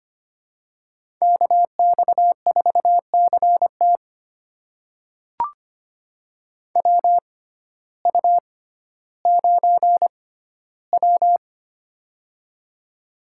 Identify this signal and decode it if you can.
Morse code, heard as a keyed tone, about 25 wpm, spaced as for 14 wpm, 700 Hz. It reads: KX4CT WU9W
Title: USA Call Signs - 6x Character Spacing - 25wpm